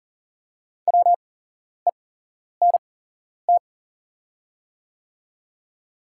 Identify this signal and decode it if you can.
Morse code, heard as a keyed tone, about 40 wpm, spaced as for 11 wpm, 690 Hz. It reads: WENT
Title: 8x Character Spacing - Top 300 Words - 40wpm